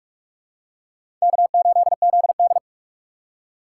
Morse code, heard as a keyed tone, about 46 wpm, 700 Hz. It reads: K8ZD
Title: USA Call Signs - 4 Characters - 6x Character Spacing - 45wpm